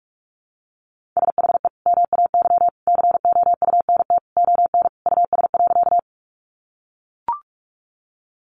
Morse code, heard as a keyed tone, about 44 wpm, 700 Hz. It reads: SHE MAY COUNT ON US.
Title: Sentences from Top 1500 Words 45wpm